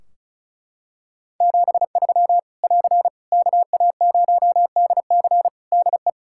Morse code, heard as a keyed tone, about 36 wpm, 700 Hz. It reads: 73 <AR> KA0DC DE
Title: Practice QSO - Exchange 4 - Ending - 35wpm